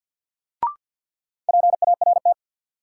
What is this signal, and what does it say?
Morse code, heard as a keyed tone, about 53 wpm, 700 Hz. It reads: PART
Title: Sets of 2 Words - New Word Drill - 50wpm